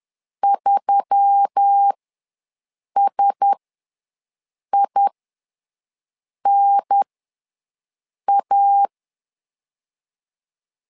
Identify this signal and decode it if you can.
Morse code, heard as a keyed tone, about 10 wpm, spaced as for 5 wpm, 800 Hz. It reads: 3SINA
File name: LEZIONI / 4 Q1S3 ws 10wpm 2437